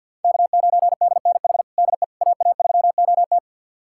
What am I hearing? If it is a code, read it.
Morse code, heard as a keyed tone, about 48 wpm, 700 Hz. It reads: K9DNH DE AA3OT
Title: USA Callsign DE Callsign - 5 Characters - 50wpm